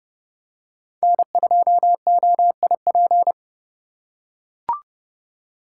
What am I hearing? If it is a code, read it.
Morse code, heard as a keyed tone, about 32 wpm, 700 Hz. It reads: N2OIP
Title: USA Call Signs - 5 Characters - 3x Character Spacing - 30wpm